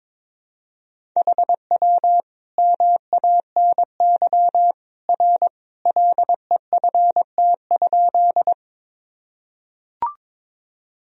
Morse code, heard as a keyed tone, about 22 wpm, 700 Hz. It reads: HW MANY R LEFT?
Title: Sentences from Top 300 Words and Abbreviations 22wpm